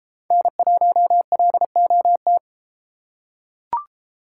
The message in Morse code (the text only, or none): N1LOT